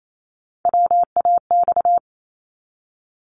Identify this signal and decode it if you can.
Morse code, heard as a keyed tone, about 28 wpm, 700 Hz. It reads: WAX
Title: Multi-letter Phonemes 28wpm